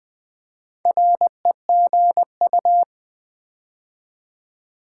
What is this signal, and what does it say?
Morse code, heard as a keyed tone, about 20 wpm, 700 Hz. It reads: REGU
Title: Build A Word - 4 Letters At A Time - 10-Letter Words - 20wpm